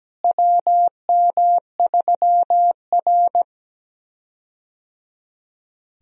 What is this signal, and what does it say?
Morse code, heard as a keyed tone, about 17 wpm, 700 Hz. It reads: WM3R
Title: USA Call Signs - 4 Characters - Repeated 3x - 17wpm